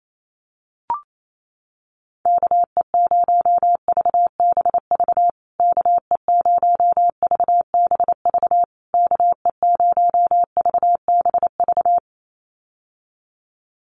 Morse code, heard as a keyed tone, about 28 wpm, 700 Hz. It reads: KE0VBV KE0VBV KE0VBV
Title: USA Call Signs Repeated 3x 28wpm